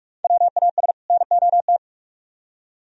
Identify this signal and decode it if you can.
Morse code, heard as a keyed tone, about 60 wpm, spaced as for 52 wpm, 700 Hz. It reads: WAS NOT
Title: Speed-Racing - Sets of 2 Words 45wpm